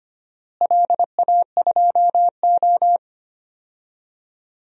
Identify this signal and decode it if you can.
Morse code, heard as a keyed tone, about 25 wpm, 700 Hz. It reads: LA2O
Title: DXCC-100 Call Signs - 4 Characters - Repeated 2x - 25wpm